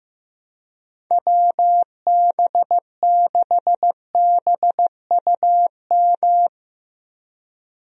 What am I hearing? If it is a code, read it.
Morse code, heard as a keyed tone, about 15 wpm, 700 Hz. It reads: WB6BUM
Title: USA Call Signs - 6 Characters - Repeated 2x - 15wpm